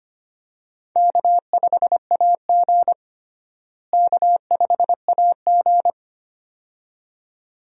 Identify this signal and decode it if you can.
Morse code, heard as a keyed tone, about 25 wpm, 700 Hz. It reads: K5AG K5AG
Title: USA Call Signs - 4 Characters - Repeated 2x - 25wpm